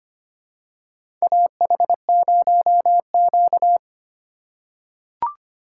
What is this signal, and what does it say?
Morse code, heard as a keyed tone, about 25 wpm, 700 Hz. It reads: AH0Q
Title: USA Call Signs - 4 Characters - 4x Character Spacing - 25wpm